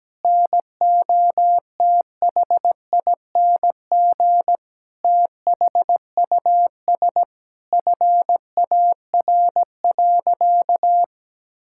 Morse code, heard as a keyed tone, about 17 wpm, 700 Hz. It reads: NOTHING THUS FAR.
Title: Sentences from Top 1000 Words - Review Part 5 - 17wpm